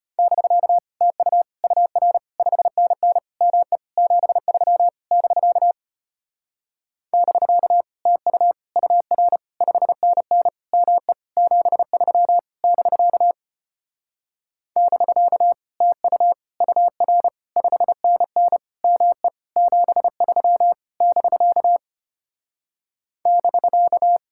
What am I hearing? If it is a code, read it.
Morse code, heard as a keyed tone, about 33 wpm, 700 Hz. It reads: <BK> TU UR 5NN ME 73 <BK> <BK> TU UR 5NN ME 73 <BK> <BK> TU UR 5NN ME 73 <BK> <BK>